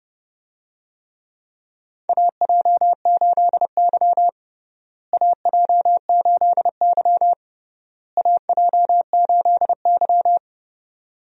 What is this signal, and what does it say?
Morse code, heard as a keyed tone, about 30 wpm, 700 Hz. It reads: AJ8Y AJ8Y AJ8Y
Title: USA Call Signs - 4 Characters - Repeated 3x - 30wpm